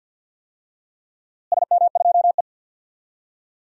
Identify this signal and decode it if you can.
Morse code, heard as a keyed tone, about 46 wpm, 700 Hz. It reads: SM2E
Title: DXCC-100 Call Signs - 4 Characters - Repeated 2x - 50wpm